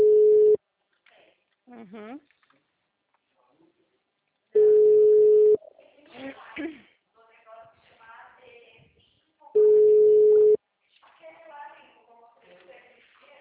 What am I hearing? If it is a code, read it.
Ringback tone (Europe, 425 Hz), with speech.
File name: out- -2002- - - .308